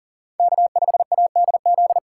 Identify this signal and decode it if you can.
Morse code, heard as a keyed tone, about 38 wpm, 700 Hz. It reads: K5ADZ
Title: USA Call Signs - 5 Characters - 2x Character Spacing - 40wpm